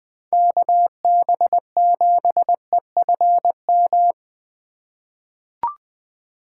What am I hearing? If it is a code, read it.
Morse code, heard as a keyed tone, about 20 wpm, 700 Hz. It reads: KB7EFM
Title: USA Call Signs - 6x Character Spacing - 20wpm